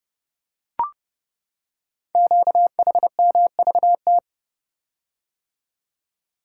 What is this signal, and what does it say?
Morse code, heard as a keyed tone, about 30 wpm, 700 Hz. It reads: QHMVT
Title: Code Groups - 5 Letters - 30wpm